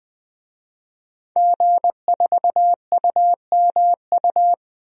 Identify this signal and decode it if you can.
Morse code, heard as a keyed tone, about 21 wpm, 700 Hz. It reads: G4UMU
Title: DXCC-100 Call Signs - 5 Characters - 20wpm